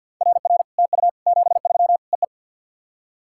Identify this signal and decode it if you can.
Morse code, heard as a keyed tone, about 50 wpm, 700 Hz. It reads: RR TU 73 EE
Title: SOTA QSO Practice 50wpm